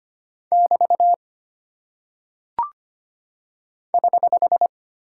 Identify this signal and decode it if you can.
Morse code, heard as a keyed tone, about 25 wpm, 700 Hz. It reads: <BT> <HH>
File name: Single Punctuation-Prosign - 25wpm